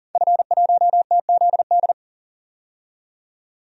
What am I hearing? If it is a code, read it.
Morse code, heard as a keyed tone, about 40 wpm, 700 Hz. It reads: F1TZD
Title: DXCC-100 Call Signs - 5 Characters - 40wpm